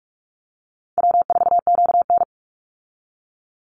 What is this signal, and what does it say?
Morse code, heard as a keyed tone, about 44 wpm, 700 Hz. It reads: W4QN
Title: Easy Call Signs Repeated 2x - 45wpm